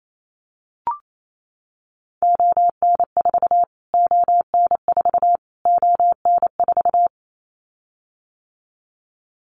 Morse code, heard as a keyed tone, about 28 wpm, 700 Hz. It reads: ON4 ON4 ON4
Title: Callsign Prefixes for Top 25 DXCC - Repeated 3x - 28wpm